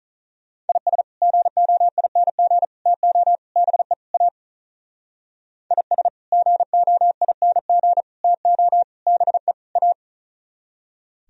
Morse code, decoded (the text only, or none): IS GOING TO BE A IS GOING TO BE A